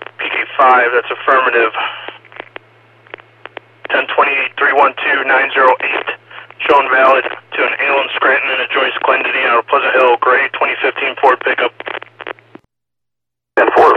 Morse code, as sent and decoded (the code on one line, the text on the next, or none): none